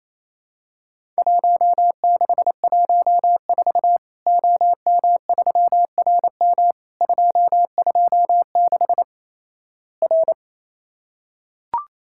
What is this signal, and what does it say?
Morse code, heard as a keyed tone, about 28 wpm, 700 Hz. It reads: 1614 OM3RM 226 R
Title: Worked All Europe DX Contest - QTC Exchange - 28wpm